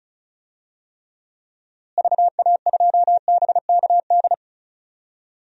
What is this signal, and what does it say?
Morse code, heard as a keyed tone, about 36 wpm, 700 Hz. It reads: VA2BKD
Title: DXCC-100 Call Signs - 6 Characters - Repeated 2x - 35wpm